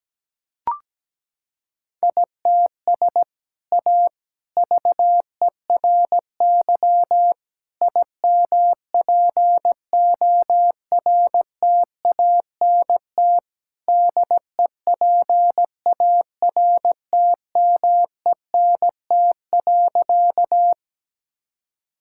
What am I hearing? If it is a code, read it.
Morse code, heard as a keyed tone, about 17 wpm, 700 Hz. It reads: ITS A VERY IMPORTANT DEPARTMENT.
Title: Sentences from Top 1000 Words - Review Part 5 - 17wpm